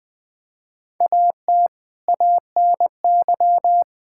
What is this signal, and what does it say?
Morse code, heard as a keyed tone, about 20 wpm, 700 Hz. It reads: AT ANY
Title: Sentences from Top 1600 Words 20wpm